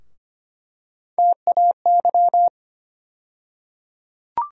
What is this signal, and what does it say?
Morse code, heard as a keyed tone, about 25 wpm, 700 Hz. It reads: TAY
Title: USA Call Sign Postfixes - 3 Letters 25wpm